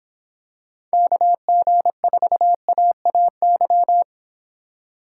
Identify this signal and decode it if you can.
Morse code, heard as a keyed tone, about 25 wpm, 700 Hz. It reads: KG4AAY